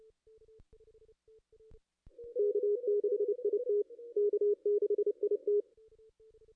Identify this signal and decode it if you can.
Morse code, heard as a keyed tone, about 29 wpm, 440 Hz. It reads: K6IT K6IT
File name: contests / ss-cw-k5tr-op-n5rz -2200